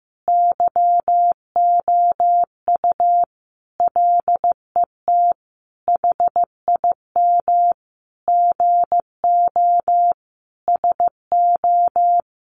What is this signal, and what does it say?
Morse code, heard as a keyed tone, about 15 wpm, 700 Hz. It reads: YOU LET HIM GO SO
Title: Sentences from Top 300 Words 15wpm